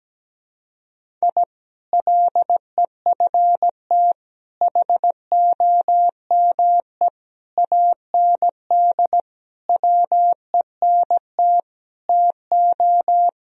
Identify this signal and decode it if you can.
Morse code, heard as a keyed tone, about 17 wpm, 700 Hz. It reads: I LEFT HOME AND WENT TO
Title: Sentences from Top 1000 Words - Review Part 3 - 17wpm